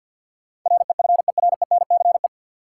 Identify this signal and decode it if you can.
Morse code, heard as a keyed tone, about 51 wpm, 700 Hz. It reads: REFERENCE
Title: Sentences from Top 1200 Words - New Word Drill - 50wpm